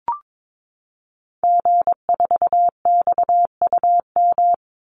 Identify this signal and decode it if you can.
Morse code, heard as a keyed tone, about 22 wpm, 700 Hz. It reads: G4XUM
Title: Call Signs - 22wpm